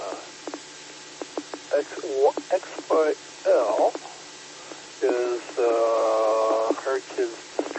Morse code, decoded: none

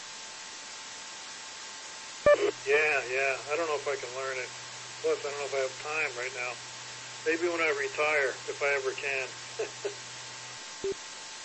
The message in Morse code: none

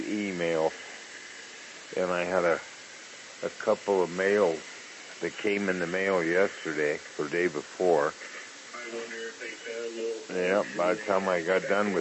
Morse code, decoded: none

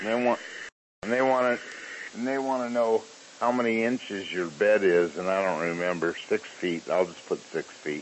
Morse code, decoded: none

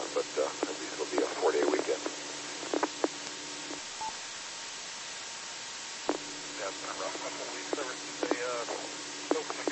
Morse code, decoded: none

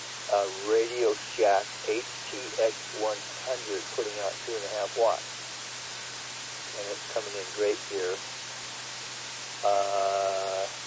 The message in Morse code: none

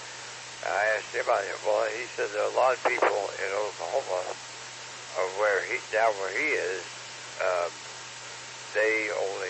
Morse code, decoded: none

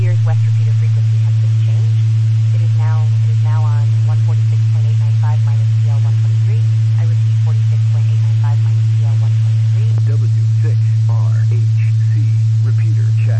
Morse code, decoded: none

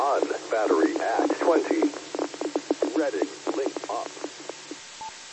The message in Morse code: none

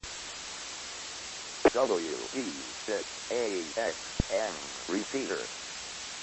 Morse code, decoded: none

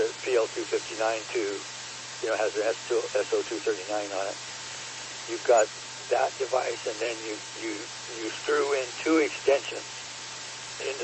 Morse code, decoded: none

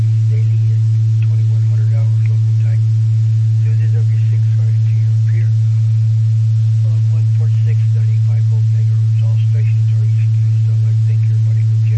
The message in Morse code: none

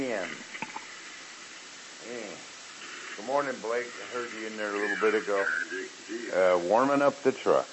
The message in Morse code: none

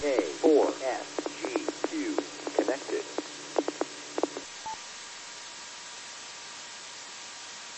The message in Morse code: none

